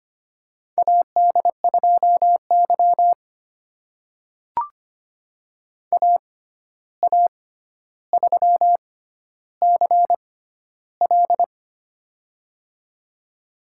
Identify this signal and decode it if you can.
Morse code, heard as a keyed tone, about 25 wpm, spaced as for 12 wpm, 700 Hz. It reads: AD2Y AA3CL
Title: USA Call Signs - 6x Character Spacing - 25wpm